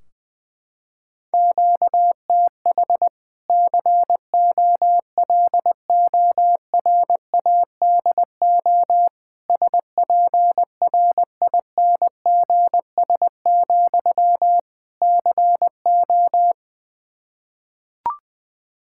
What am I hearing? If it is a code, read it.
Morse code, heard as a keyed tone, about 20 wpm, 700 Hz. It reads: QTH COLORADO SPRINGS, CO
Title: Practice QSO - Exchange 1 - Beginning - 20wpm